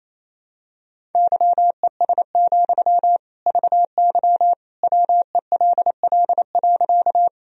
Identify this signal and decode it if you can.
Morse code, heard as a keyed tone, about 28 wpm, 700 Hz. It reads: YES, VY WELL.
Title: Sentences from Top 500 Words and Abbreviations 28wpm